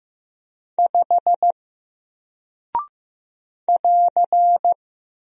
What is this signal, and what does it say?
Morse code, heard as a keyed tone, about 15 wpm, 700 Hz. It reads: 5 <AR>
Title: Single Character (Letter:Number:Punctuation:Prosign) Alt 15wpm